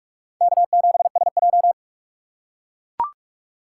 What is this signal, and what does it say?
Morse code, heard as a keyed tone, about 44 wpm, 700 Hz. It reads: K7SJ
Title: USA Call Signs - 4 Characters - 6x Character Spacing - 45wpm